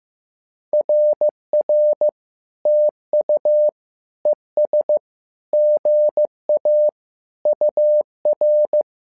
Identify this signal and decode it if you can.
Morse code, heard as a keyed tone, about 15 wpm, 600 Hz. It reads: RR TU ES GA UR
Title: SOTA QSO Practice 15wpm